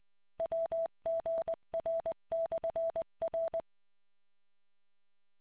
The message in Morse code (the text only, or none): WGR/R